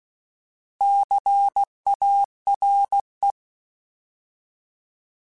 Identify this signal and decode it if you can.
Morse code, heard as a keyed tone, about 16 wpm, 800 Hz. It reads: CARE
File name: Alphabetical / Alphabetical mp3s / Cs @15